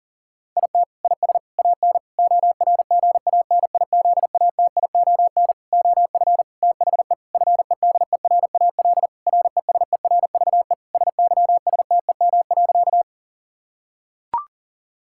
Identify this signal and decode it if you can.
Morse code, heard as a keyed tone, about 37 wpm, 700 Hz. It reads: IT IS AN ORGANIZATION OF THE FEDERAL RESERVE SYSTEM.
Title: Sentences from Top 1000 Words - Review Part 6 - 40wpm